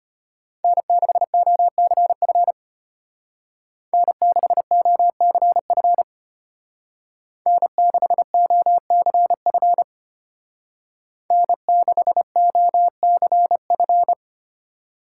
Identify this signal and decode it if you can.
Morse code, heard as a keyed tone, about 32 wpm, 700 Hz. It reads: N6OCF N6OCF N6OCF N6OCF